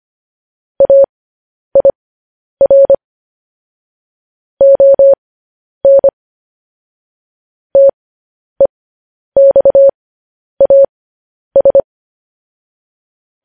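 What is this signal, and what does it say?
Morse code, heard as a keyed tone, about 25 wpm, spaced as for 10 wpm, 550 Hz. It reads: AIR ON TEXAS
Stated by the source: Morse code news headlines sent at 10 words per minute.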